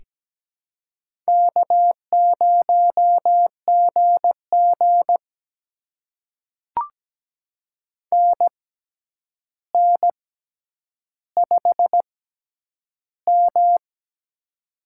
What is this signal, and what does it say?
Morse code, heard as a keyed tone, about 17 wpm, spaced as for 9 wpm, 700 Hz. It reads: K0GG NN5M
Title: USA Call Signs - 4 Characters - 6x Character Spacing - 17wpm